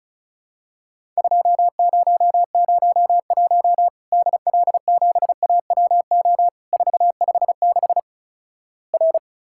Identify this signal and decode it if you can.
Morse code, heard as a keyed tone, about 36 wpm, 700 Hz. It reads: 2001 DL7AWO 456 R
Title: Worked All Europe DX Contest - QTC Exchange - 35wpm